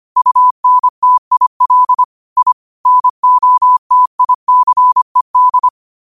Morse code, heard as a keyed tone, about 25 wpm, 1000 Hz. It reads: UNTIL I NOTICED